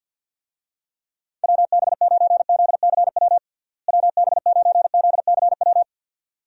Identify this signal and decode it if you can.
Morse code, heard as a keyed tone, about 48 wpm, 700 Hz. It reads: WB9ZCW WB9ZCW
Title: USA Call Signs - 6 Characters - Repeated 2x - 50wpm